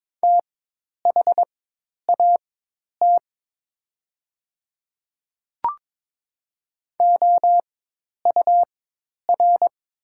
Morse code, heard as a keyed tone, about 22 wpm, spaced as for 10 wpm, 700 Hz. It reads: THAT OUR
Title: 4x Character Spacing - Top 200 Words - 22wpm